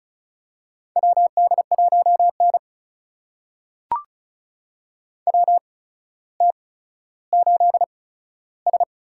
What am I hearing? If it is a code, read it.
Morse code, heard as a keyed tone, about 36 wpm, spaced as for 15 wpm, 700 Hz. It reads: WD1N WT8S
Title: USA Call Signs - 4 Characters - 8x Character Spacing - 35wpm